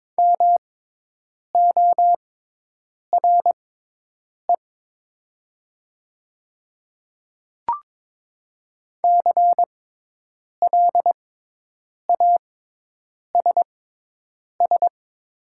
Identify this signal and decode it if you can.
Morse code, heard as a keyed tone, about 22 wpm, spaced as for 8 wpm, 700 Hz. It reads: MORE CLASS
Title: 6x Character Spacing - Top 500 Words - 22wpm